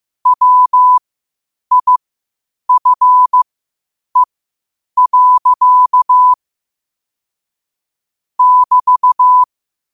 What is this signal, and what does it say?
Morse code, heard as a keyed tone, about 15 wpm, spaced as for 8 wpm, 1000 Hz. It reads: WIFE. <BT>
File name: qotdfiles / 8wpm